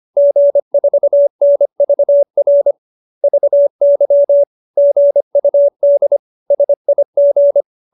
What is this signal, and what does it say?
Morse code, heard as a keyed tone, about 25 wpm, 590 Hz. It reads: G4NVR VY GUD SIG
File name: QSO315 25